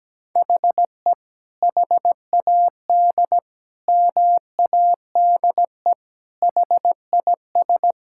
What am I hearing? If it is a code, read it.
Morse code, heard as a keyed tone, about 17 wpm, 700 Hz. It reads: HE HAD MADE HIS
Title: Sentences from Top 1600 Words 17wpm